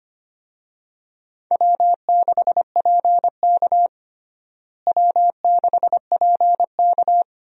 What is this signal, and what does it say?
Morse code, heard as a keyed tone, about 25 wpm, 700 Hz. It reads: W6PK W6PK
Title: USA Call Signs - 4 Characters - Repeated 2x - 25wpm